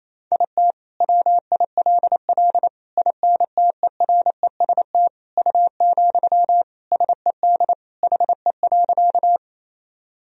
Morse code, heard as a keyed tone, about 27 wpm, 700 Hz. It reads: IT WILL INTEREST U, SED HE.
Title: Sentences from Top 400 Words and Abbreviations 28wpm